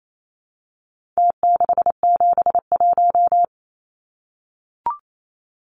Morse code, heard as a keyed tone, about 28 wpm, 700 Hz. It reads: T671
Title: Numbers - Contest Style - 4-Digit 28wpm